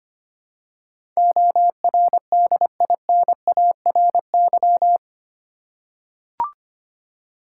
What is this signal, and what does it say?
Morse code, heard as a keyed tone, about 25 wpm, 700 Hz. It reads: ORDINARY
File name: Eight Letter Words - 25wpm